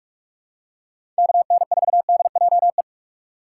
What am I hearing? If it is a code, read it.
Morse code, heard as a keyed tone, about 46 wpm, 700 Hz. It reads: KN4DJE
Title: USA Call Signs - 6 Characters - 2x Character Spacing - 45wpm